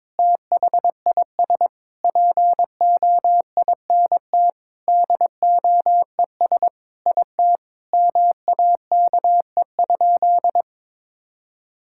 Morse code, heard as a keyed tone, about 22 wpm, 700 Hz. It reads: THIS POINT DOES IT MAKE?
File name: practice / 22wpm / Sentences with 8-Word Phrases - 22wpm